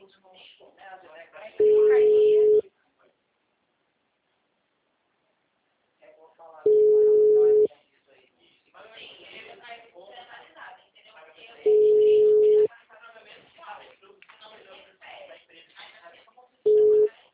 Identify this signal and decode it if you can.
Ringback tone (Europe, 425 Hz), with speech.
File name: out- -2013- - - .98591